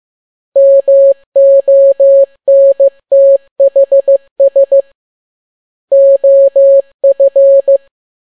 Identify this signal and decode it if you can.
Morse code, heard as a keyed tone, about 15 wpm, 550 Hz. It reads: MONTHS OF
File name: n6tt p1 15C13